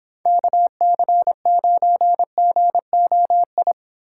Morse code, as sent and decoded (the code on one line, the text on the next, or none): -.- -.-. ----. --. --- ..
KC9GOI